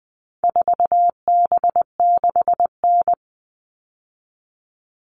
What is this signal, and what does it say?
Morse code, heard as a keyed tone, about 20 wpm, 700 Hz. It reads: 4B6N
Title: All Cut Numbers - 4-Digit 20wpm